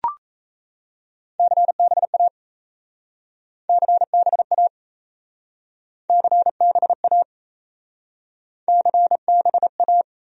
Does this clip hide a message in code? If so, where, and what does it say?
Yes: Morse code from 1.4 to 10.0 s, heard as a keyed tone, about 37 wpm, 700 Hz. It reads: CBA CBA CBA CBA